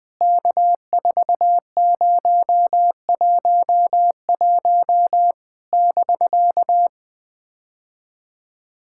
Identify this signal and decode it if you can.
Morse code, heard as a keyed tone, about 20 wpm, 700 Hz. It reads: K4011 <BK>
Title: POTA QSO Practice - 20wpm